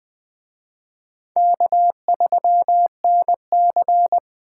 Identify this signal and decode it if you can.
Morse code, heard as a keyed tone, about 20 wpm, 700 Hz. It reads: K3NC
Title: Speed Racing - USA 1x2 Call Signs 17wpm